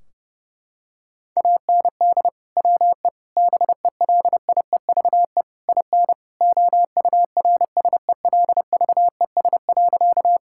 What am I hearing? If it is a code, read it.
Morse code, heard as a keyed tone, about 32 wpm, 700 Hz. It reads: AND WE BELIEVE IN OURSELVES.
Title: Sentences from Top 1600 Words 30wpm